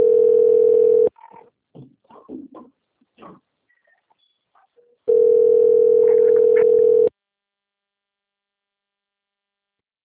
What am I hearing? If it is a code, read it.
Ringback tone (North America), 440 + 480 Hz.
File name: out- - - - - .30443